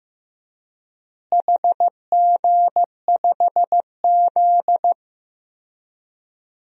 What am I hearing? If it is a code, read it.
Morse code, heard as a keyed tone, about 15 wpm, 700 Hz. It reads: HG5Z
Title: DXCC-100 Call Signs - 4 Characters - Repeated 2x - 15wpm